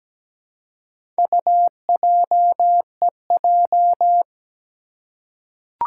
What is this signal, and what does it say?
Morse code, heard as a keyed tone, about 17 wpm, 700 Hz. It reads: UJEJ
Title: Four Challenging Letter-Number Combinations 17wpm